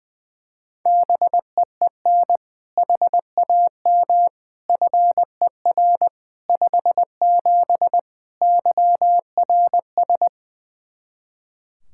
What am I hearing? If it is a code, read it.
Morse code, heard as a keyed tone, about 20 wpm, 700 Hz. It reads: BEEN HAM FER 57 YRS
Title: K1USN SST QSO Practice 20wpm